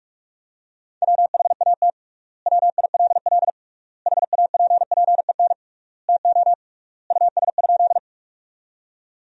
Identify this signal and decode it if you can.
Morse code, heard as a keyed tone, about 42 wpm, 700 Hz. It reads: WHAT WILL HAPPEN TO US?
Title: 3x Word-Spacing - Sentences from Top 400 Words - 45wpm